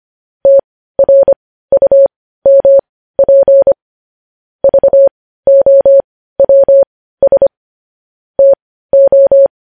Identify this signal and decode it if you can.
Morse code, heard as a keyed tone, about 25 wpm, spaced as for 15 wpm, 550 Hz. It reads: TRUMP VOWS TO
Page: news headlines in morse code at 15 wpm